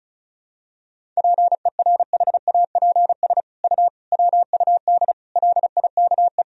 Morse code, heard as a keyed tone, about 36 wpm, 700 Hz. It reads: PERHAPS U WUD LIKE
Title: Sentences from Top 500 Words and Abbreviations 35wpm